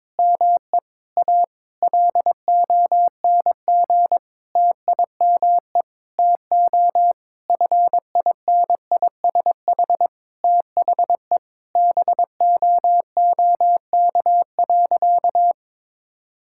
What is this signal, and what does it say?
Morse code, heard as a keyed tone, about 22 wpm, 700 Hz. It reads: ME A LONG TIME TO FINISH THE BOOK.
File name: Sentences with 7-Word Phrases - 22wpm